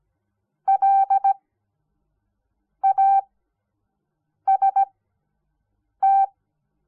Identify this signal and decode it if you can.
Morse code, heard as a keyed tone, about 17 wpm, spaced as for 6 wpm, 780 Hz. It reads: LAST
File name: Onlinelehrgang / morsen / DJ4UF Morsen06